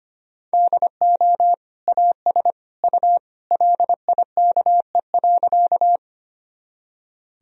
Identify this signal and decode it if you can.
Morse code, heard as a keyed tone, about 25 wpm, 700 Hz. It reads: DO AS U LIKE.
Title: Sentences from Top 100 Words and Abbreviations 25wpm